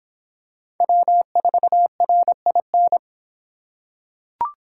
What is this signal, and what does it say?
Morse code, heard as a keyed tone, about 24 wpm, 700 Hz. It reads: W4RIN